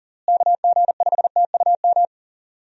Morse code, heard as a keyed tone, about 40 wpm, 700 Hz. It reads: KG5TUM
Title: USA Call Signs - 6 Characters - Repeated 3x - 40wpm